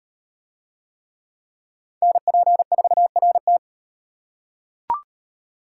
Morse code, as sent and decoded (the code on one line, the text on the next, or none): -. .--. ....- .-. -
NP4RT